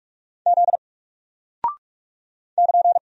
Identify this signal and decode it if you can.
Morse code, heard as a keyed tone, about 44 wpm, 700 Hz. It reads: Z <KN>
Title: Single Character (Letter:Number:Punctuation:Prosign) Alt 45wpm